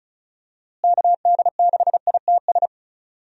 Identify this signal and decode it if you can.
Morse code, heard as a keyed tone, about 36 wpm, 700 Hz. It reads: KD6ITS
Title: USA Call Signs - 6 Characters - Repeated 2x - 35wpm